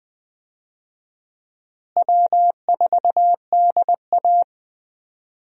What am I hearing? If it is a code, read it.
Morse code, heard as a keyed tone, about 20 wpm, 700 Hz. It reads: W4DA
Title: USA Call Signs - 4 Characters - Repeated 2x - 20wpm